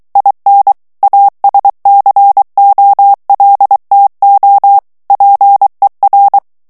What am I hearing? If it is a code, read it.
Morse code, heard as a keyed tone, about 23 wpm, 800 Hz. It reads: IN ASCOLTO PER
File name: morse mp3 / ITALIEN texte clair 1 vitesse 22 mots minute